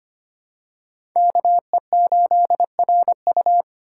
Morse code, heard as a keyed tone, about 25 wpm, 700 Hz. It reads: KE8RU
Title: K1USN SST QSO Practice - 25wpm